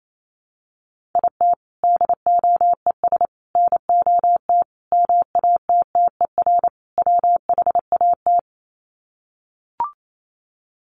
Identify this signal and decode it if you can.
Morse code, heard as a keyed tone, about 28 wpm, 700 Hz. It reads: IT DOES NOT MATTER WHAT
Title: Sets of 5 Words Encore 28wpm